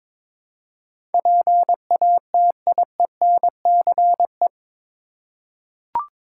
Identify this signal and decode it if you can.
Morse code, heard as a keyed tone, about 22 wpm, 700 Hz. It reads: PATIENCE
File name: practice / 22wpm / Eight Letter Words - 22wpm